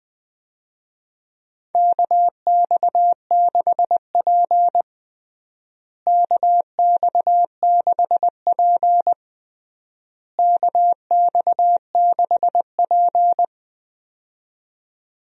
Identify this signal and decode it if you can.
Morse code, heard as a keyed tone, about 20 wpm, 700 Hz. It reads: KX6P KX6P KX6P
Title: USA Call Signs - 4 Characters - Repeated 3x - 20wpm